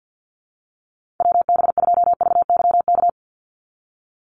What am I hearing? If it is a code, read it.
Morse code, heard as a keyed tone, about 50 wpm, 700 Hz. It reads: WB2VYK
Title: Call Signs - 50wpm